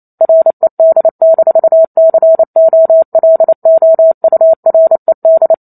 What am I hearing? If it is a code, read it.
Morse code, heard as a keyed tone, about 28 wpm, 650 Hz. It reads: RED-COLOURED